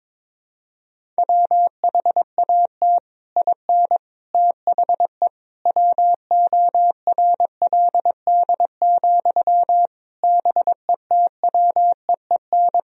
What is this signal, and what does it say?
Morse code, heard as a keyed tone, about 22 wpm, 700 Hz. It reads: WHAT IN THE WORLD, BETWEEN
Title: Sentences from Top 1000 Words - Review Part 1 - 22wpm